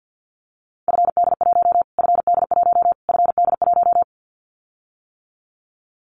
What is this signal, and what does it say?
Morse code, heard as a keyed tone, about 50 wpm, 700 Hz. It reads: FD1 FD1 FD1
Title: Callsign Prefixes for Top 25 DXCC - Repeated 3x - 50wpm